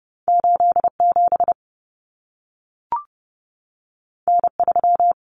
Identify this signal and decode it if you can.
Morse code, heard as a keyed tone, about 30 wpm, 700 Hz. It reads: 87 N3
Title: Common Cut Numbers - 2-Digit 30wpm